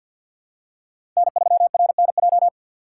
Morse code, heard as a keyed tone, about 48 wpm, 700 Hz. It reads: N3RNJ
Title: USA Call Signs - 5 Characters - 4x Character Spacing - 50wpm